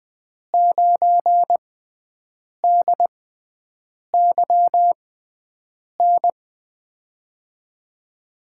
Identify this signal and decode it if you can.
Morse code, heard as a keyed tone, about 20 wpm, spaced as for 7 wpm, 700 Hz. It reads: 9DYN
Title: USA Call Signs - 6 Characters - 6x Character Spacing - 20wpm